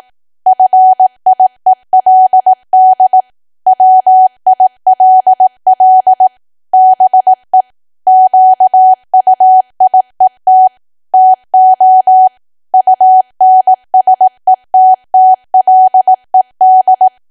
Morse code, heard as a keyed tone, about 18 wpm, 750 Hz. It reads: FIELD WILL BE QUIET TO UNSETTLED
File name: cwa / W1AW-Level2 / 18WPM